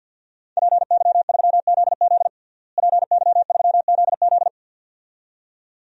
Morse code, heard as a keyed tone, about 50 wpm, 700 Hz. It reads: PY3ZZ PY3ZZ
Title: DXCC-100 Call Signs - 5 Characters - Repeated 2x - 50wpm